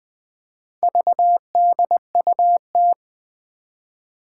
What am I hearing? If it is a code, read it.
Morse code, heard as a keyed tone, about 20 wpm, 700 Hz. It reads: VDUT
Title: Four Challenging Letter-Number Combinations 20wpm